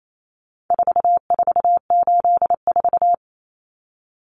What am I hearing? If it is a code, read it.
Morse code, heard as a keyed tone, about 28 wpm, 700 Hz. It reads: 4484
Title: Common Cut Numbers - 4-Digit 28wpm